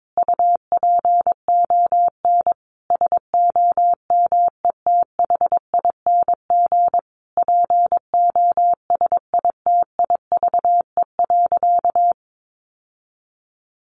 Morse code, heard as a keyed tone, about 22 wpm, 700 Hz. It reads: UPON SOMETHING POSITIVE.
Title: Sentences from Top 1400 Words 22wpm